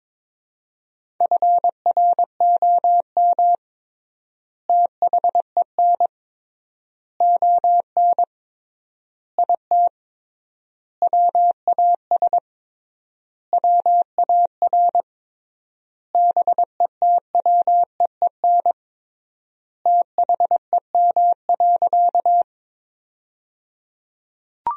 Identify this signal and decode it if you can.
Morse code, heard as a keyed tone, about 22 wpm, 700 Hz. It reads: FROM THEN ON IT WAS WAR BETWEEN THEM.
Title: 3x Word-Spacing - Sentences from Top 400 Words - 22wpm